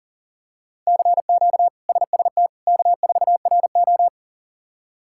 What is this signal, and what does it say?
Morse code, heard as a keyed tone, about 40 wpm, 700 Hz. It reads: CQ SST K4RO
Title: K1USN SST QSO Practice - 40wpm